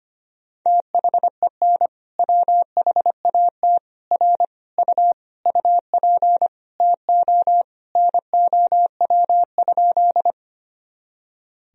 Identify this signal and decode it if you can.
Morse code, heard as a keyed tone, about 25 wpm, 700 Hz. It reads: THEN WHAT R U UP TO NOW?
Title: Sentences from Top 100 Words and Abbreviations 25wpm